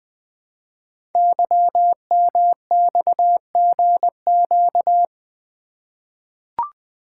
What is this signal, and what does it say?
Morse code, heard as a keyed tone, about 20 wpm, 700 Hz. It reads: YMXGQ
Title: Code Groups - 5 Letters - 20wpm